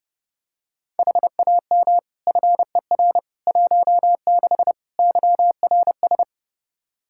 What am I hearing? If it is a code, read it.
Morse code, heard as a keyed tone, about 30 wpm, 700 Hz. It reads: HAM FER 16 YRS
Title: Practice QSO - Exchange 3 - More Info - 30wpm